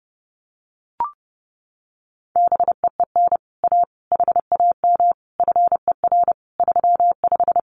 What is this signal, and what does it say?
Morse code, heard as a keyed tone, about 30 wpm, 700 Hz. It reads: BEEN A HAM FER 35
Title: Common QSO Phrases Encore 30wpm